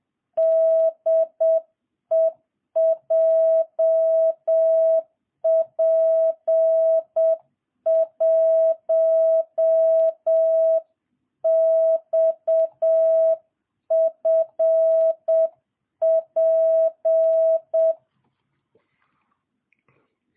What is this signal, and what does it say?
Morse code, heard as a keyed tone, about 7 wpm, 660 Hz. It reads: DE JP1XFP